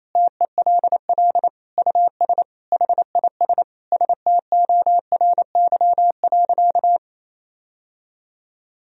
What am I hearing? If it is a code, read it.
Morse code, heard as a keyed tone, about 28 wpm, 700 Hz. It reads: TELL US HIS STORY.
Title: Sentences from Top 1000 Words - Review Part 1 - 28wpm